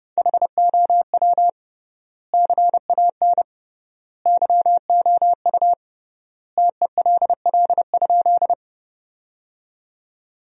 Morse code, heard as a keyed tone, about 30 wpm, 700 Hz. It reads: HOW CAN YOU TELL?
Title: 3x Word-Spacing - Sentences from Top 200 Words - 30wpm